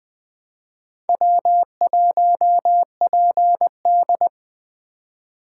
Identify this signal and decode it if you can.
Morse code, heard as a keyed tone, about 24 wpm, 700 Hz. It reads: W1PD
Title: Speed Racing - USA 1x2 Call Signs 20wpm